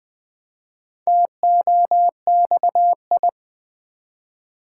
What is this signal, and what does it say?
Morse code, heard as a keyed tone, about 20 wpm, 700 Hz. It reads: TOXI
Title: Build A Word - 1 Letter At A Time - 5-Letter Words - 20wpm